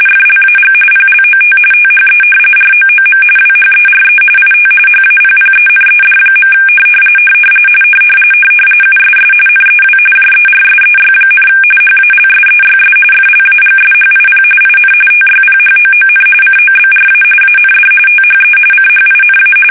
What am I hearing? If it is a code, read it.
RTTY, about 45.45 baud. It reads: FLAG PLOT THE ROOM WHERE IT HAPPENED. USS NEW JERSEY, HALSEYS FLAGSHIP IN TYPHOON COBRA AND THE CHINA
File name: fsk850 / battleshipnj